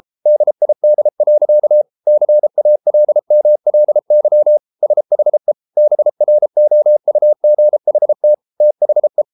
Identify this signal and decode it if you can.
Morse code, heard as a keyed tone, about 33 wpm, 600 Hz. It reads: DID. CALMLY SHE BROUGHT THE